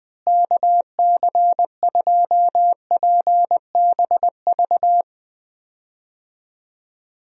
Morse code, heard as a keyed tone, about 20 wpm, 700 Hz. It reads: KC2PBV
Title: USA Call Signs - 6 Characters - Repeated 3x - 20wpm